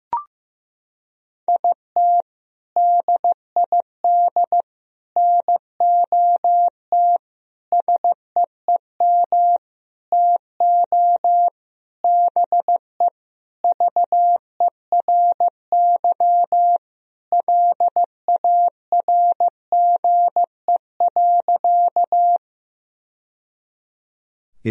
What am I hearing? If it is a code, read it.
Morse code, heard as a keyed tone, about 15 wpm, 700 Hz. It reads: IT DID NOT SEEM TO BE VERY LARGE.
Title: Sentences from Top 1000 Words - Review Part 1 - 15wpm